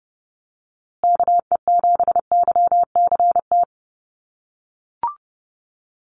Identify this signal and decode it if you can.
Morse code, heard as a keyed tone, about 30 wpm, 700 Hz. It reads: KE7YCT
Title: USA Call Signs Repeated 2x 30wpm